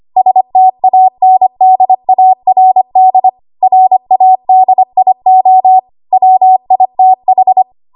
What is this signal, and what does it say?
Morse code, heard as a keyed tone, about 25 wpm, 750 Hz. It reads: STANDARD RADIO WITH
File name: Morse / 25WPM